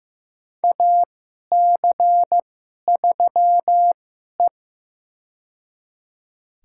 Morse code, heard as a keyed tone, about 15 wpm, spaced as for 11 wpm, 700 Hz. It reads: AC3E
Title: USA Call Signs - 4 Characters - 2x Character Spacing - 15wpm